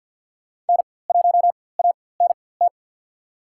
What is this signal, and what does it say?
Morse code, heard as a keyed tone, about 53 wpm, spaced as for 24 wpm, 700 Hz. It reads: N1ANT
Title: USA Call Signs - 5 Characters - 4x Character Spacing - 50wpm